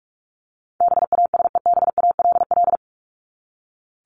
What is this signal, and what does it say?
Morse code, heard as a keyed tone, about 44 wpm, 700 Hz. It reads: BASEBALL
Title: Top 1751-2000 Words 45wpm